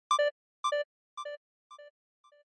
There is no code.